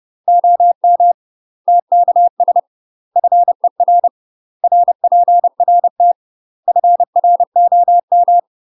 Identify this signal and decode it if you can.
Morse code, heard as a keyed tone, about 30 wpm, 700 Hz. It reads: OM TKS FER RPRT FROM
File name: UK MASTER QSOs TRAINING / 30WPM / UK MASTER QSOs 0001